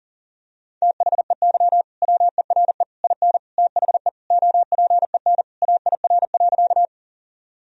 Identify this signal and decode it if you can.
Morse code, heard as a keyed tone, about 40 wpm, 700 Hz. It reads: THEY WERE IN THE OPEN AIR.
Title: Sentences from Top 1000 Words - Review Part 1 - 40wpm